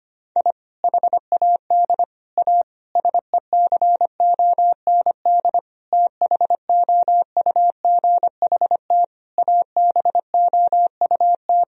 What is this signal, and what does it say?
Morse code, heard as a keyed tone, about 25 wpm, 700 Hz. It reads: I HAD A SECOND THOUGHT ABOUT
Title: Sentences from Top 1000 Words - Review Part 1 - 25wpm